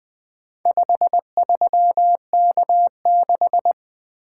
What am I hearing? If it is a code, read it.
Morse code, heard as a keyed tone, about 20 wpm, 700 Hz. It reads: 53K6
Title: Four Letter-Number Combinations - 20wpm